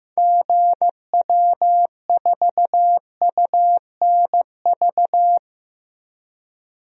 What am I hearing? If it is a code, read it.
Morse code, heard as a keyed tone, about 15 wpm, 700 Hz. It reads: GW4UNV
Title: DXCC-100 Call Signs - 6 Characters - Repeated 3x - 15wpm